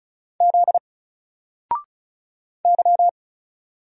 Morse code, heard as a keyed tone, about 36 wpm, 700 Hz. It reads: Z Y
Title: Single Character (Letter:Number:Punctuation:Prosign) Alt 35wpm